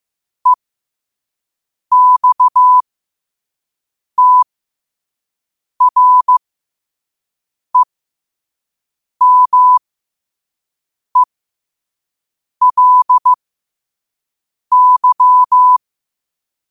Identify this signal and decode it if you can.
Morse code, heard as a keyed tone, about 15 wpm, spaced as for 5 wpm, 1000 Hz. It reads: EXTREMELY
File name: qotdfiles / 5wpm